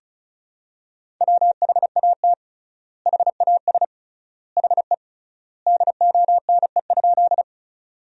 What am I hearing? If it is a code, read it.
Morse code, heard as a keyed tone, about 36 wpm, 700 Hz. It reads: WHAT HAS HE DONE?
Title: 3x Word-Spacing - Sentences from Top 500 Words - 35wpm